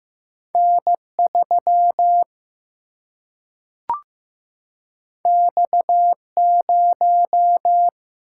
Morse code, heard as a keyed tone, about 15 wpm, 700 Hz. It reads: N 3 X 0
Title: Single-Letter Single-Number 15wpm